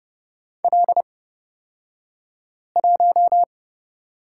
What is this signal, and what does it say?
Morse code, heard as a keyed tone, about 29 wpm, spaced as for 5 wpm, 700 Hz. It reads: L1
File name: Lesson 38d - Letter q - Two Characters 30wpm